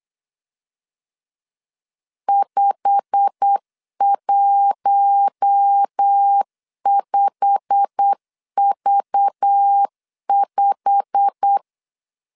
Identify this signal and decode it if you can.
Morse code, heard as a keyed tone, about 8 wpm, 800 Hz. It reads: 515V5
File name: LEZIONI / 8 YJ45 / 8 YJ45 8wpm 3214